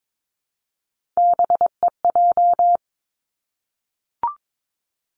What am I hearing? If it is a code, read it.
Morse code, heard as a keyed tone, about 22 wpm, 700 Hz. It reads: BEJ
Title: Three Challenging Letter-Number Combinations 22wpm